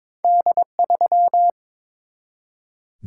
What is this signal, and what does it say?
Morse code, heard as a keyed tone, about 22 wpm, 700 Hz. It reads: D3
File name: practice / 22wpm / Two Letter-Number Combinations - 22wpm